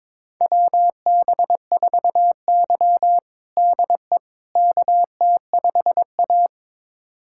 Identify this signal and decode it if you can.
Morse code, heard as a keyed tone, about 22 wpm, 700 Hz. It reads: WB4Y DE KT5A
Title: USA Callsign DE Callsign - 4 Characters - 22wpm